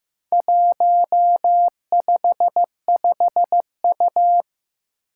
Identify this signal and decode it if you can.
Morse code, heard as a keyed tone, about 15 wpm, 700 Hz. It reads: 155U
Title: Four Challenging Letter-Number Combinations 15wpm